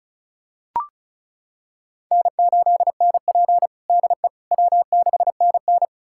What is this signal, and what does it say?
Morse code, heard as a keyed tone, about 36 wpm, 700 Hz. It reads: N8NP DE W6NN
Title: USA Callsign DE Callsign - 4 Characters - 35wpm